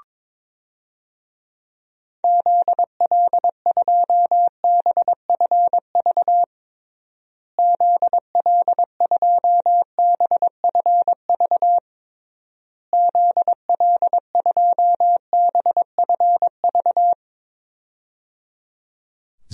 Morse code, heard as a keyed tone, about 22 wpm, 700 Hz. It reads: ZL2BFV ZL2BFV ZL2BFV
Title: DXCC-100 Call Signs - 6 Characters - Repeated 3x - 22wpm